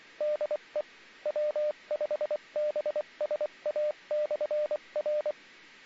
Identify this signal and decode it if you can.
Morse code, heard as a keyed tone, about 24 wpm, 600 Hz, with radio noise. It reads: DE W5BSA/R